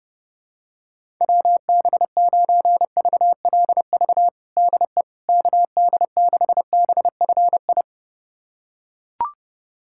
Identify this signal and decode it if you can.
Morse code, heard as a keyed tone, about 30 wpm, 700 Hz. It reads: WB9VLV DE KD6BFI
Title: USA Callsign DE Callsign 30wpm